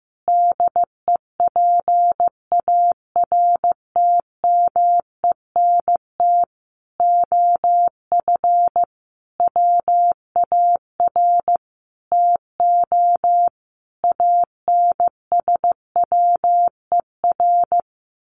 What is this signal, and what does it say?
Morse code, heard as a keyed tone, about 15 wpm, 700 Hz. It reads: DEPARTMENT OF WAR TO ANSWER
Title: Sentences from Top 1500 Words 15wpm